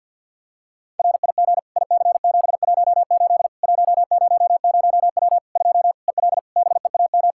Morse code, heard as a keyed tone, about 49 wpm, 700 Hz. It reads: RIG IC718 100W 2 EL BEAM
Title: Practice QSO - Exchange 2 - Info - 50wpm